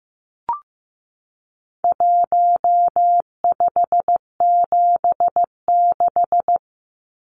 Morse code, heard as a keyed tone, about 15 wpm, 700 Hz. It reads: 1576
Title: Numbers - Contest Style - 4-Digit 15wpm